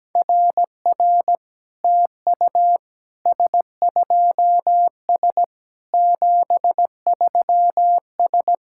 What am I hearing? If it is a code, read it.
Morse code, heard as a keyed tone, about 17 wpm, 700 Hz. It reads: RR TU S2S 73S
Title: SOTA QSO Practice 17wpm